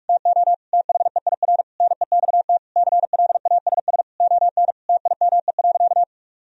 Morse code, heard as a keyed tone, about 41 wpm, 700 Hz. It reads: TO THEIR NEXT CLASS ON TIME.
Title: Sentences from Top 1000 Words - Review Part 2 - 45wpm